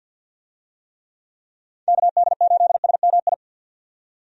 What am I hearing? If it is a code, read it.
Morse code, heard as a keyed tone, about 52 wpm, 700 Hz. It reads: KD8SMI
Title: USA Call Signs - 6 Characters - 50wpm